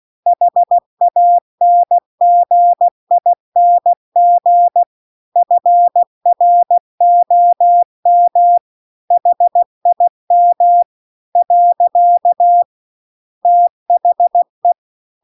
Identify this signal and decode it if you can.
Morse code, heard as a keyed tone, about 16 wpm, 700 Hz. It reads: HANGING FROM HIM . THE
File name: CW-The-Big-Twoheaded-River-2-16WPM